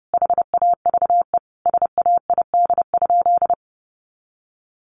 Morse code, heard as a keyed tone, about 30 wpm, 700 Hz. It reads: HAVE SAID?
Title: Sentences from Top 1500 Words 30wpm